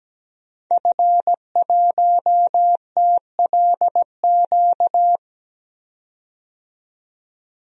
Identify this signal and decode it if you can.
Morse code, heard as a keyed tone, about 17 wpm, 700 Hz. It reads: F1TLQ
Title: DXCC-100 Call Signs - 5 Characters - 17wpm